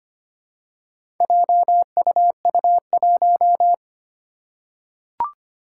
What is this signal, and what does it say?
Morse code, heard as a keyed tone, about 25 wpm, 700 Hz. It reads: JUU1
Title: Four Challenging Letter-Number Combinations 25wpm